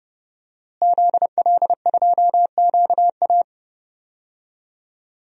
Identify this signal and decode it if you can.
Morse code, heard as a keyed tone, about 29 wpm, 700 Hz. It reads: ZL2QA
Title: DXCC-100 Call Signs - 5 Characters - Repeated 3x - 30wpm